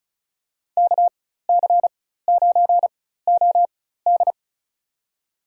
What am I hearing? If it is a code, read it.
Morse code, heard as a keyed tone, about 34 wpm, spaced as for 16 wpm, 700 Hz. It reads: KC9OD
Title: USA Call Signs - 5 Characters - 4x Character Spacing - 35wpm